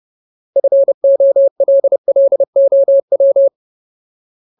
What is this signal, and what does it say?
Morse code, heard as a keyed tone, about 30 wpm, 550 Hz. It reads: FOLLOW
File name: Ultimate Word List / 30WPM Ultimate